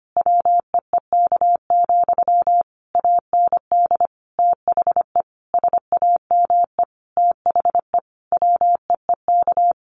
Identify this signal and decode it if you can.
Morse code, heard as a keyed tone, about 25 wpm, 700 Hz. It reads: WEEK, AND THE SAME THE WEEK
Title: Sentences from Top 500 Words 25wpm